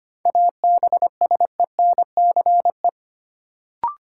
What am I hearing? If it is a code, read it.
Morse code, heard as a keyed tone, about 25 wpm, 700 Hz. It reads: ABSENCE
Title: Sets of 4 Words - New Word Drill - 25wpm